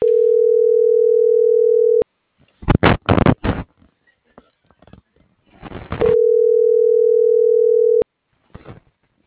Ringback tone (North America), 440 + 480 Hz.
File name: CENTRAL TERRESTRE / out vane.lazaro - .34569